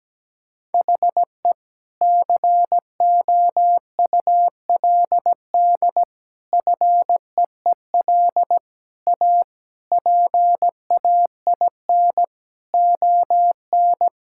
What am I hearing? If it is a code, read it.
Morse code, heard as a keyed tone, about 17 wpm, 700 Hz. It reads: HE COULD FEEL A PAIN ON
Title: Sentences from Top 1600 Words 17wpm